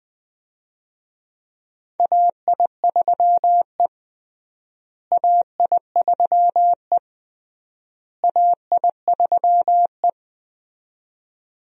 Morse code, heard as a keyed tone, about 21 wpm, 700 Hz. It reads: AI3E AI3E AI3E
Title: USA Call Signs - 4 Characters - Repeated 3x - 20wpm